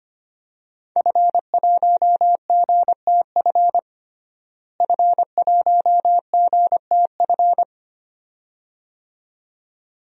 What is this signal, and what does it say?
Morse code, heard as a keyed tone, about 25 wpm, 700 Hz. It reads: F1GTF F1GTF
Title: DXCC-100 Call Signs - 5 Characters - Repeated 2x - 25wpm